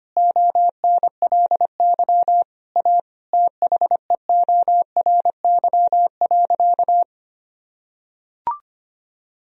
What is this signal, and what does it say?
Morse code, heard as a keyed tone, about 25 wpm, 700 Hz. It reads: ONLY A THEORY.
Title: Sentences from Top 1000 Words - Review Part 3 - 25wpm